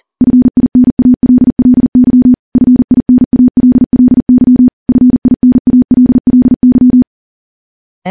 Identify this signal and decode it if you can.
Morse code, heard as a keyed tone, about 42 wpm, 250 Hz. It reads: FINALLY FINALLY FINALLY
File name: morse / morse-training-40WPM